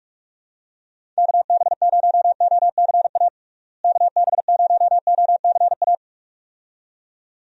Morse code, heard as a keyed tone, about 46 wpm, 700 Hz. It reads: KB0OCA KB0OCA
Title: USA Call Signs - 6 Characters - Repeated 2x - 45wpm